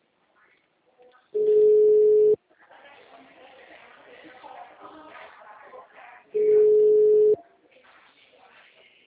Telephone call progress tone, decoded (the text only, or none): ringback tone